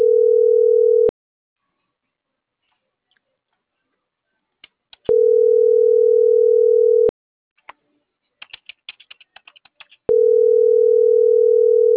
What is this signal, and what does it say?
Ringback tone (North America), 440 + 480 Hz.